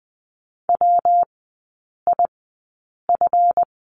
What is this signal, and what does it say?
Morse code, heard as a keyed tone, about 20 wpm, spaced as for 8 wpm, 690 Hz. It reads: WIF
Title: 166 Lesson 27e - Number 7 - Three Characters 20wpm